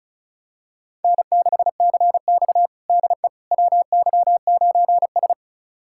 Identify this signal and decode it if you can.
Morse code, heard as a keyed tone, about 36 wpm, 700 Hz. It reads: N6CX DE WY9S
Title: USA Callsign DE Callsign - 4 Characters - 35wpm